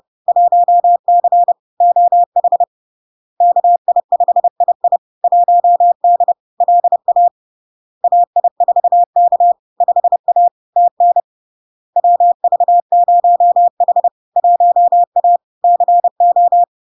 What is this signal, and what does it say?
Morse code, heard as a keyed tone, about 30 wpm, 700 Hz. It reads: 1C OH KI5II 1D LA AI4K 5A TN WV0H 1A CO
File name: LARK-FD-2022-Log-in-CW-30wpm-700-Hz